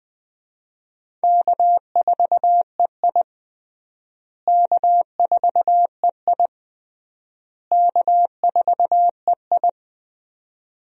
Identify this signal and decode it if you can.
Morse code, heard as a keyed tone, about 20 wpm, 700 Hz. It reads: K4EI K4EI K4EI
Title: USA Call Signs - 4 Characters - Repeated 3x - 20wpm